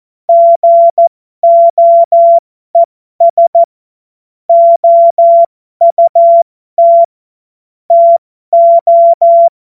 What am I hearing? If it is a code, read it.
Morse code, heard as a keyed tone, about 14 wpm, spaced as for 12 wpm, 680 Hz. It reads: GOES OUT TO
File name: Morsekurs / Witze